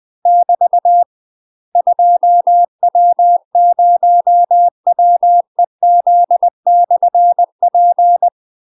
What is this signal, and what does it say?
Morse code, heard as a keyed tone, about 20 wpm, 700 Hz. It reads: <BT> 2W0WEZ/P
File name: UK MASTER QSOs TRAINING / 20WPM / UK MASTER QSOs 0011